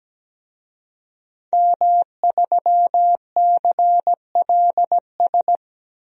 Morse code, heard as a keyed tone, about 17 wpm, 700 Hz. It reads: M3CLS
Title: DXCC-100 Call Signs - 5 Characters - 17wpm